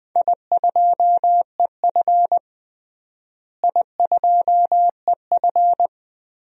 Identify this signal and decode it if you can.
Morse code, heard as a keyed tone, about 21 wpm, 700 Hz. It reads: I2EF I2EF
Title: DXCC-100 Call Signs - 4 Characters - Repeated 3x - 20wpm